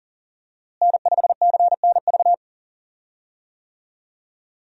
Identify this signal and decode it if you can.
Morse code, heard as a keyed tone, about 40 wpm, 700 Hz. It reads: N5CNV
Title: USA Call Signs - 5 Characters - Repeated 3x - 40wpm